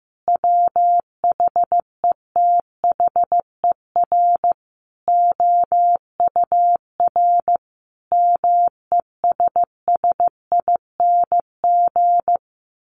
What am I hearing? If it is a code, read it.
Morse code, heard as a keyed tone, about 15 wpm, 700 Hz. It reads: WHETHER OUR MESSING
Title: A Sound Of Thunder 15wpm